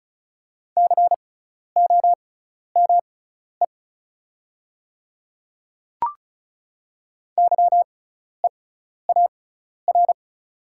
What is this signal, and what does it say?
Morse code, heard as a keyed tone, about 34 wpm, spaced as for 12 wpm, 700 Hz. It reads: COME YEAR
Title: 6x Character Spacing - Top 200 Words - 35wpm